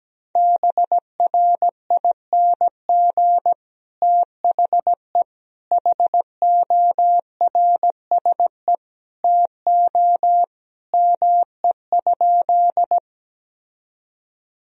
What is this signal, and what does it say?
Morse code, heard as a keyed tone, about 17 wpm, 700 Hz. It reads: BRING THE HORSE TO ME?
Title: Sentences from Top 1000 Words - Review Part 3 - 17wpm